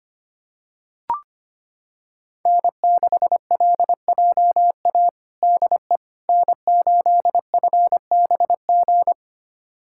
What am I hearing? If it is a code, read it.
Morse code, heard as a keyed tone, about 25 wpm, 700 Hz. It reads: N6LJA DE N8FBG
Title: USA Callsign DE Callsign - 5 Characters - 25wpm